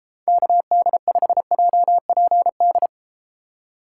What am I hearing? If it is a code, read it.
Morse code, heard as a keyed tone, about 32 wpm, 700 Hz. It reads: KD5JPD